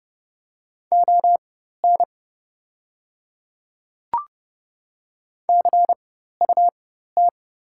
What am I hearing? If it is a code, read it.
Morse code, heard as a keyed tone, about 30 wpm, spaced as for 14 wpm, 700 Hz. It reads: ON CUT
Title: 4x Character Spacing - Top 500 Words - 30wpm